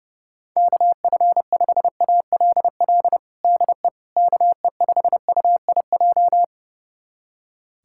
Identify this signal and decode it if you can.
Morse code, heard as a keyed tone, about 32 wpm, 700 Hz. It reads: KF5ALL DE KE5UIJ
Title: USA Callsign DE Callsign 30wpm